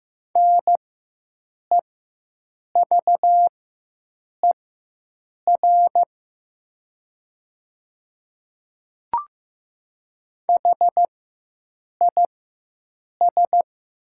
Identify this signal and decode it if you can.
Morse code, heard as a keyed tone, about 15 wpm, spaced as for 7 wpm, 700 Hz. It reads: NEVER HIS
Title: 4x Character Spacing - Top 300 Words - 15wpm